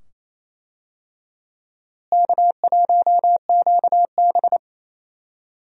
Morse code, heard as a keyed tone, about 27 wpm, 700 Hz. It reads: K1QB
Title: USA Call Signs - 4 Characters - Repeated 2x - 28wpm